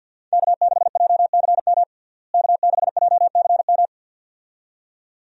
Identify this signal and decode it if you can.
Morse code, heard as a keyed tone, about 49 wpm, 700 Hz. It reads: K6JCM K6JCM
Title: USA Call Signs - 5 Characters - Repeated 2x - 50wpm